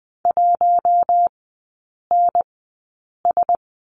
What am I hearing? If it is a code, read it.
Morse code, heard as a keyed tone, about 20 wpm, spaced as for 8 wpm, 700 Hz. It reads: 1NS
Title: 041 Lesson 8e - Number 1 - Three Characters 20wpm